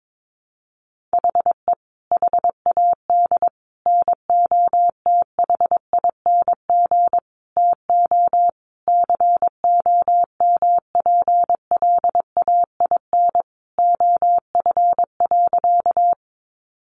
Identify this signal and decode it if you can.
Morse code, heard as a keyed tone, about 22 wpm, 700 Hz. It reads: HE HAD NOTHING TO COMPLAIN OF.
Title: Sentences from Top 1300 Words 22wpm